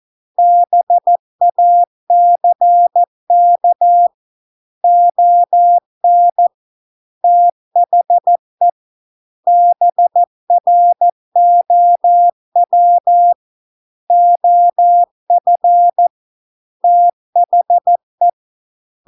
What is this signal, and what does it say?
Morse code, heard as a keyed tone, about 14 wpm, 700 Hz. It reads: BACK ON THE BROW OF THE